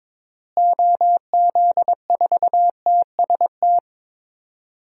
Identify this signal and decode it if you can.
Morse code, heard as a keyed tone, about 22 wpm, 700 Hz. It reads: OZ4TST
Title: DXCC-100 Call Signs - 6 Characters - Repeated 3x - 22wpm